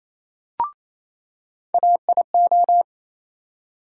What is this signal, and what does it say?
Morse code, heard as a keyed tone, about 28 wpm, 700 Hz. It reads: AIO
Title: USA Call Sign Postfixes - 3 Letters 28wpm